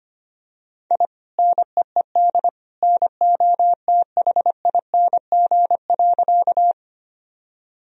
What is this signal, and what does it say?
Morse code, heard as a keyed tone, about 25 wpm, 700 Hz. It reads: I NEED NOTHING.
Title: Sentences from Top 1000 Words - Review Part 1 - 25wpm